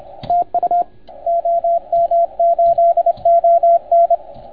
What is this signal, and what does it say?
Morse code, heard as a keyed tone, about 29 wpm, 690 Hz, with radio noise. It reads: TU OM8ON
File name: Archiv / CUC287 / OK1FGD CUC 287